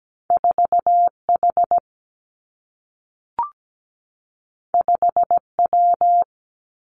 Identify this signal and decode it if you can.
Morse code, heard as a keyed tone, about 17 wpm, 700 Hz. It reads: 4 H 5 W
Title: Two Challenging Letter-Number Combinations 17wpm